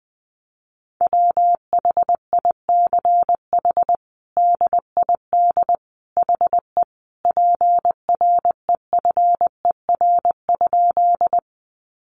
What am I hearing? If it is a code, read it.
Morse code, heard as a keyed tone, about 20 wpm, 700 Hz. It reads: WHICH DID HE PREFER?
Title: Sentences from Top 1100 Words 20wpm